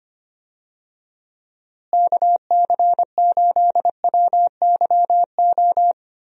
Morse code, heard as a keyed tone, about 25 wpm, 700 Hz. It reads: KC8WYO
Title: USA Call Signs - 6 Characters - Repeated 2x - 25wpm